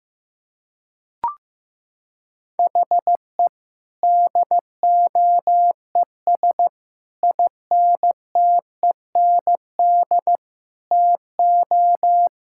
Morse code, heard as a keyed tone, about 16 wpm, 700 Hz. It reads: HE DOES INTEND TO
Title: Sentences from Top 1600 Words 15wpm